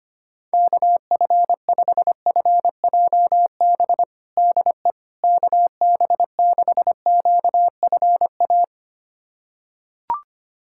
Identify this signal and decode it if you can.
Morse code, heard as a keyed tone, about 25 wpm, 700 Hz. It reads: KF5FJB DE KB6QFA
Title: USA Callsign DE Callsign - 6 Characters - 25wpm